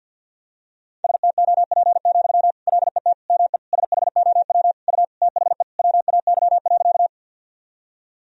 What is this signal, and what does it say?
Morse code, heard as a keyed tone, about 52 wpm, 700 Hz. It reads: STOP, LET ME SHOW U THE WAY.
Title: Sentences from Top 300 Words and Abbreviations 50wpm